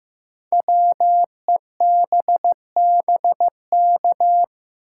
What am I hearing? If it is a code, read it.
Morse code, heard as a keyed tone, about 15 wpm, 700 Hz. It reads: WEBBK
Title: Code Groups - 5 Letter:Numbers - 15wpm